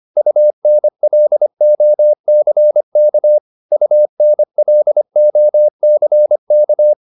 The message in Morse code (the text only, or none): UNLOCK UNLOCK